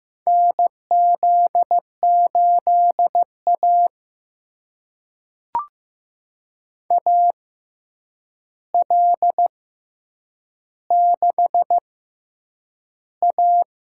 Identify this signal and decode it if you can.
Morse code, heard as a keyed tone, about 15 wpm, spaced as for 8 wpm, 700 Hz. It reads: NZ8A AL6A
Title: USA Call Signs - 4 Characters - 6x Character Spacing - 15wpm